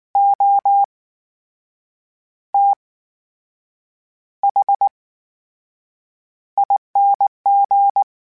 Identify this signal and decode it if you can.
Morse code, heard as a keyed tone, about 19 wpm, spaced as for 7 wpm, 800 Hz. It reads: OTHING
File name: Special INGs / Special INGs-mp3s / VerySlowToFaster Ing @18